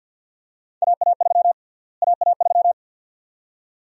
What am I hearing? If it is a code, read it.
Morse code, heard as a keyed tone, about 52 wpm, 700 Hz. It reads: AA3 AA3
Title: USA Call Sign Prefixes - Repeated 2x - 50wpm